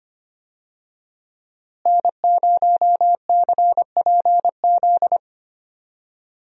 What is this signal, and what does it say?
Morse code, heard as a keyed tone, about 25 wpm, 700 Hz. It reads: N0CPZ
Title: USA Call Signs - 5 Characters - Repeated 3x - 25wpm